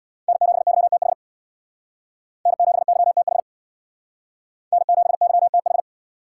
Morse code, heard as a keyed tone, about 62 wpm, 700 Hz. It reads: N7YTH N7YTH N7YTH
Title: Speed Racing - USA 1x3 Call Signs 50wpm